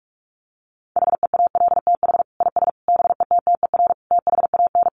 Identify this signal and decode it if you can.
Morse code, heard as a keyed tone, about 44 wpm, 700 Hz. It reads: HEALTH IS BETTER THAN
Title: Sentences from Top 1200 Words 45wpm